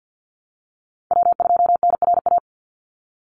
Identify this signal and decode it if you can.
Morse code, heard as a keyed tone, about 50 wpm, 700 Hz. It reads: W2NRA
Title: Call Signs Repeated 3x - 50wpm